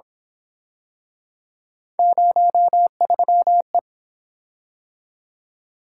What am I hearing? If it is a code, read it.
Morse code, heard as a keyed tone, about 26 wpm, 700 Hz. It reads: 03E
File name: Speed-Racing - Three Letter-Number Combinations - 22wpm